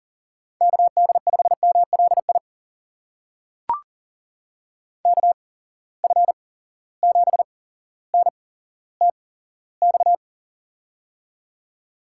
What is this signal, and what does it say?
Morse code, heard as a keyed tone, about 40 wpm, spaced as for 17 wpm, 700 Hz. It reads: KD5MLI KF7NTX
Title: USA Call Signs - 8x Character Spacing - 40wpm